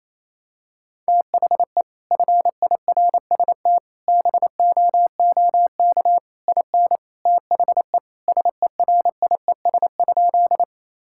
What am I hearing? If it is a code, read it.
Morse code, heard as a keyed tone, about 28 wpm, 700 Hz. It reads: THE FIRST BOOK IN THE SERIES?
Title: Sentences from Top 1000 Words - Review Part 5 - 28wpm